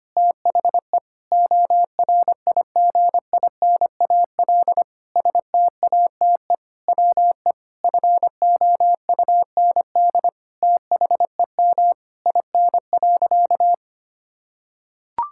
Morse code, heard as a keyed tone, about 25 wpm, 700 Hz. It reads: THE ORIGINAL STATE WE FOUND THEM IN.
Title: Sentences from Top 1000 Words - Review Part 6 - 25wpm